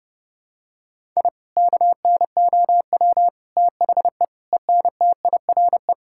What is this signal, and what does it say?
Morse code, heard as a keyed tone, about 32 wpm, 700 Hz. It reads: I KNOW THE ENTIRE
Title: Sentences from Top 1000 Words - Review Part 5 - 30wpm